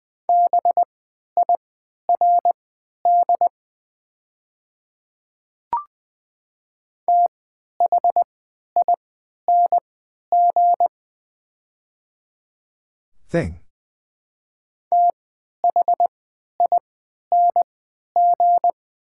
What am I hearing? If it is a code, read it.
Morse code, heard as a keyed tone, about 20 wpm, spaced as for 11 wpm, 700 Hz. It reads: BIRD THING THING
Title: 3x Character Spacing - Top 500 Words - 20wpm